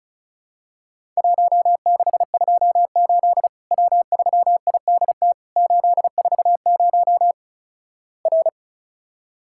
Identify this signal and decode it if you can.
Morse code, heard as a keyed tone, about 36 wpm, 700 Hz. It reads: 1628 W3IDT 840 R
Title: Worked All Europe DX Contest - QTC Exchange - 35wpm